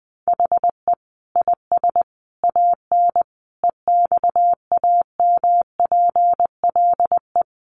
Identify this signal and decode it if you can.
Morse code, heard as a keyed tone, about 20 wpm, 700 Hz. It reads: HE IS AN EXAMPLE
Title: Sentences from Top 1500 Words 20wpm